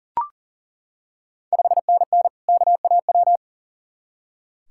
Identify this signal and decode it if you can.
Morse code, heard as a keyed tone, about 40 wpm, 690 Hz. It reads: 5NN KAW
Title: Ontario QSO Party Exchange 40wpm